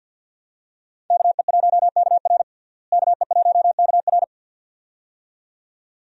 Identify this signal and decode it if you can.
Morse code, heard as a keyed tone, about 47 wpm, 700 Hz. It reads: KE1KR KE1KR
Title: USA Call Signs - 5 Characters - Repeated 2x - 50wpm